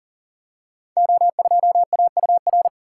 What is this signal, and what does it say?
Morse code, heard as a keyed tone, about 38 wpm, 700 Hz. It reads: O2AUR
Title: Code Groups - 5 Letter:Numbers - 40wpm